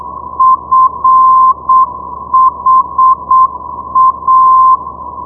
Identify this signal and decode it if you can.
Morse code, heard as a keyed tone, about 7 wpm, 1040 Hz. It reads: FHA
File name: ndb friedrichshafen fha 473.0